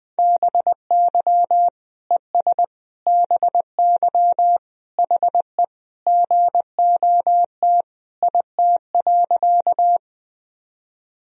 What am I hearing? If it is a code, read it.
Morse code, heard as a keyed tone, about 21 wpm, 700 Hz. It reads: BY ES BY HE GOT IT.
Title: Sentences from Top 300 Words and Abbreviations 20wpm